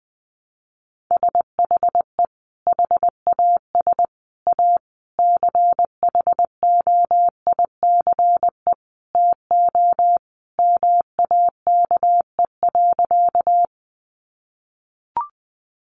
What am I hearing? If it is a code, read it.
Morse code, heard as a keyed tone, about 20 wpm, 700 Hz. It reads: SHE HAS A CHOICE TO MAKE.
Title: Sentences from Top 900 Words 20wpm